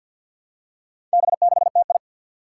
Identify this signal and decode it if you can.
Morse code, heard as a keyed tone, about 47 wpm, 700 Hz. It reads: B6TI
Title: Four Challenging Letter-Number Combinations 50wpm